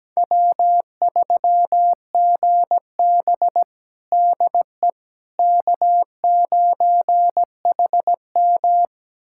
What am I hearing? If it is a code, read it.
Morse code, heard as a keyed tone, about 17 wpm, 700 Hz. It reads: W3GB DE K9HM
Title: USA Callsign DE Callsign - 4 Characters - 17wpm